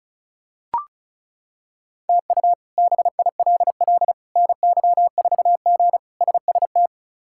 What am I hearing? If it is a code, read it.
Morse code, heard as a keyed tone, about 36 wpm, 700 Hz. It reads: TU BILL NY4G SST
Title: K1USN SST QSO Practice - 35wpm